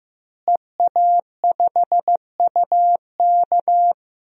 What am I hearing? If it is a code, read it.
Morse code, heard as a keyed tone, about 16 wpm, 700 Hz. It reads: EA5UK
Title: DXCC-100 Call Signs - 5 Characters - Repeated 3x - 15wpm